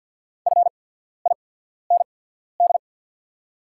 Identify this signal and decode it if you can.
Morse code, heard as a keyed tone, about 50 wpm, spaced as for 14 wpm, 700 Hz. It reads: FIND
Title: 8x Character Spacing - Top 100 Words - 50wpm